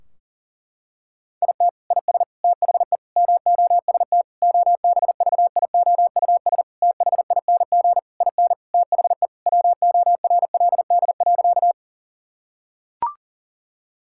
Morse code, heard as a keyed tone, about 40 wpm, 700 Hz. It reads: IT IS THE MOST OBVIOUS THING IN THE WORLD.
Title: Sentences from Top 1600 Words 40wpm